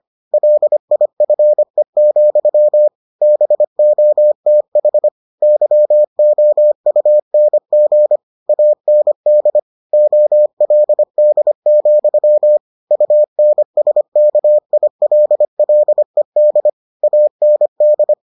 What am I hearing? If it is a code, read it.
Morse code, heard as a keyed tone, about 25 wpm, 600 Hz. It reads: LIFE, BOTH YOUNG AND OLD, UNSKILLED AND